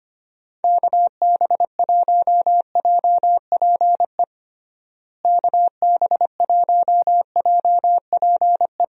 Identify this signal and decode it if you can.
Morse code, heard as a keyed tone, about 25 wpm, 700 Hz. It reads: KB1JPE KB1JPE
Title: USA Call Signs - 6 Characters - Repeated 3x - 25wpm